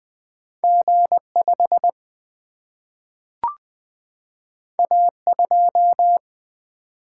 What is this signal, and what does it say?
Morse code, heard as a keyed tone, about 20 wpm, 700 Hz. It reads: G 5 A 2
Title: Single-Letter Single-Number 20wpm